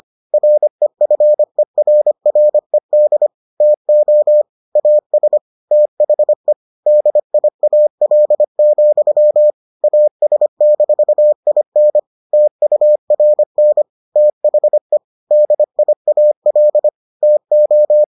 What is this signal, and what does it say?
Morse code, heard as a keyed tone, about 25 wpm, 600 Hz. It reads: REFERRED TO AS THE DIAL, AS-IN TURN THE DIAL TO